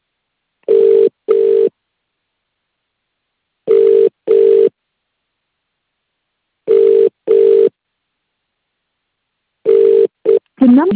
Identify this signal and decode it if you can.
Ringback tone (UK), 400 + 450 Hz, with speech.